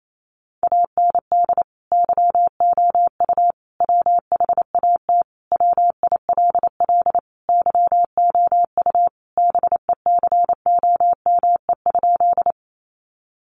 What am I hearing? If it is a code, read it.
Morse code, heard as a keyed tone, about 28 wpm, 700 Hz. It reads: AND YOU WHAT WILL YOU BECOME?
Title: Sentences from Top 600 Words 28wpm